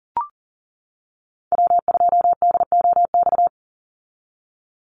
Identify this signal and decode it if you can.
Morse code, heard as a keyed tone, about 40 wpm, 700 Hz. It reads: W2DOX
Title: USA Call Signs 40wpm